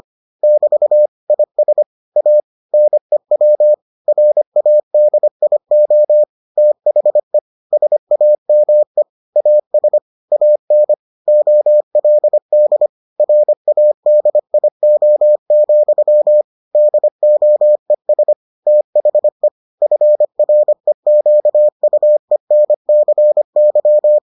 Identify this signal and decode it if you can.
Morse code, heard as a keyed tone, about 25 wpm, 600 Hz. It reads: <BT> IS A NEW RADIO THE SAME AS AN OLD RADIO, DOES THE FREQUENCY